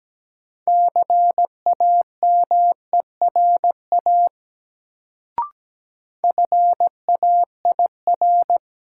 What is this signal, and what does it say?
Morse code, heard as a keyed tone, about 17 wpm, 700 Hz. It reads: CAMERA FAIR
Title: Sentences from Top 1000 Words - Review Part 5 - New Word Drill - 17wpm